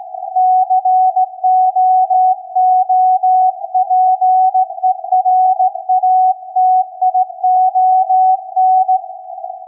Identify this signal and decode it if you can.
Morse code, heard as a keyed tone, about 16 wpm, 740 Hz. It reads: COOPERATION
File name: iz7slz / VLF / SAQ 16-11-2022